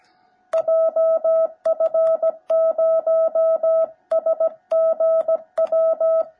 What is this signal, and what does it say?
Morse code, heard as a keyed tone, about 17 wpm, 650 Hz. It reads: JF0SGW
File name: jh8cbh / 2021cw / cw17b